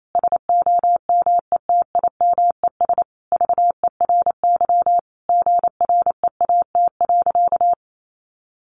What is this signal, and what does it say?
Morse code, heard as a keyed tone, about 28 wpm, 700 Hz. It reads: SOMETIMES VERY GREAT.
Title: Sentences from Top 1200 Words 28wpm